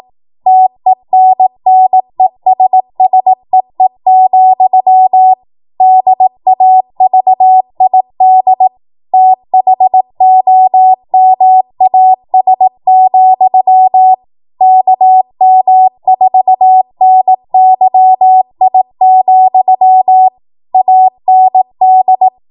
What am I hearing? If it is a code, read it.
Morse code, heard as a keyed tone, about 18 wpm, 750 Hz. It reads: TENNESSEE, DAVID THOMAS, KM4NYI, AND
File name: Morse / 18WPM